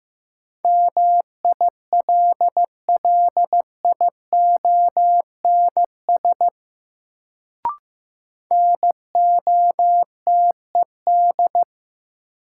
Morse code, heard as a keyed tone, about 15 wpm, 700 Hz. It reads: MILLIONS NOTED
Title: Sets of 5 Words - New Word Drill - 15wpm